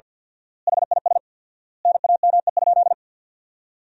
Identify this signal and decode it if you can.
Morse code, heard as a keyed tone, about 50 wpm, 700 Hz. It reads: HIS NAME?
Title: 4x Word-Spacing - Sentences from Top 400 Words - 50wpm